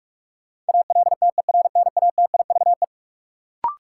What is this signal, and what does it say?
Morse code, heard as a keyed tone, about 43 wpm, 700 Hz. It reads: ALTERNATIVE
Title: Sentences from Top 1000 Words - Review Part 4 - New Word Drill - 45wpm